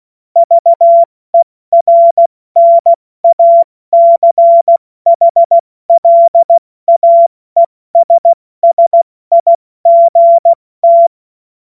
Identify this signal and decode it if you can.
Morse code, heard as a keyed tone, about 16 wpm, 680 Hz. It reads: VERNACHLAESSIGT
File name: Morsekurs / Witze / Witz 21. Juni 2022 70 F80